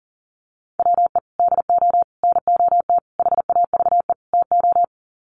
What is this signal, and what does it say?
Morse code, heard as a keyed tone, about 40 wpm, 700 Hz. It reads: WE DO NOT HAVE TO
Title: Sets of 5 Words 40wpm